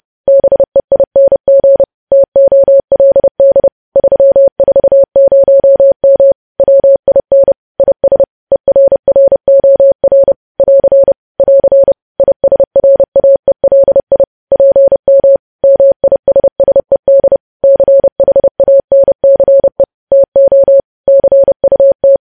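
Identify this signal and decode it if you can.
Morse code, heard as a keyed tone, about 30 wpm, 550 Hz. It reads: BEING TOLD 340M WIN IS ERROR <AR> <AR> ISRAELI PM MISSED CHANCE TO CUT
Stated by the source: Morse code news headlines practice sent at 30 wpm.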